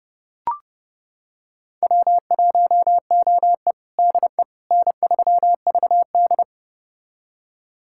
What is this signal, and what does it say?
Morse code, heard as a keyed tone, about 32 wpm, 700 Hz. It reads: W1OE DE N3VD
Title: USA Callsign DE Callsign - 4 Characters - 30wpm